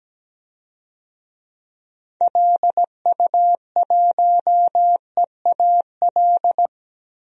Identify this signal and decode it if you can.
Morse code, heard as a keyed tone, about 17 wpm, 700 Hz. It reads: LU1EAL
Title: DXCC-100 Call Signs - 6 Characters - 17wpm